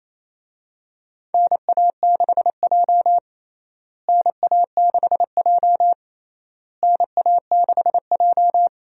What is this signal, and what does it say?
Morse code, heard as a keyed tone, about 28 wpm, 700 Hz. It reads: NA6J NA6J NA6J
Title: USA Call Signs - 4 Characters - Repeated 3x - 28wpm